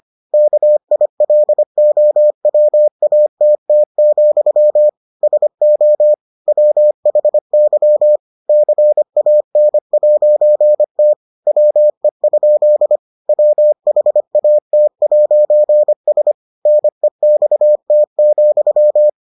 Morse code, heard as a keyed tone, about 25 wpm, 600 Hz. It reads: KILOWATT, SO WHY CAN'T WE? WHAT'S NEXT,